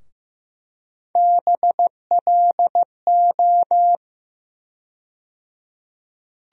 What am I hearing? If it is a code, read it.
Morse code, heard as a keyed tone, about 15 wpm, 700 Hz. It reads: BLO
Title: USA Call Sign Postfixes - 3 Letters 15wpm